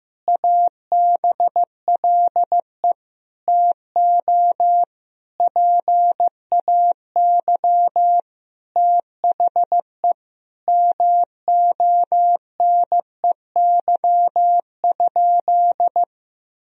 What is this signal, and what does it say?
Morse code, heard as a keyed tone, about 16 wpm, 700 Hz. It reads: ABLE TO PAY THE MONEY?
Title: Sentences from Top 1000 Words - Review Part 2 - 15wpm